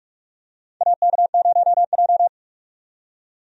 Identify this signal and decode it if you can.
Morse code, heard as a keyed tone, about 47 wpm, 700 Hz. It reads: AK0J
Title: USA Call Signs - 4 Characters - 45wpm